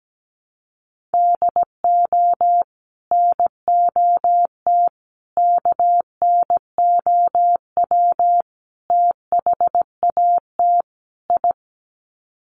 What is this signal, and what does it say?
Morse code, heard as a keyed tone, about 17 wpm, 700 Hz. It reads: DO NOT KNOW THAT I
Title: Sets of 5 Words Encore2 17wpm